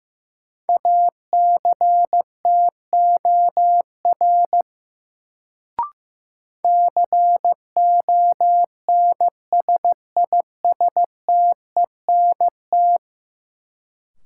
Morse code, heard as a keyed tone, about 16 wpm, 700 Hz. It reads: ACTOR CONSISTENT
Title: Sentences from Top 1500 Words - New Word Drill Alt - 15wpm